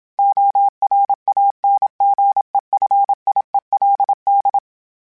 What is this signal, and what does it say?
Morse code, heard as a keyed tone, about 27 wpm, 800 Hz. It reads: ORANGEFIELD
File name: fromSMs / FromSMs mp3s / Cities@25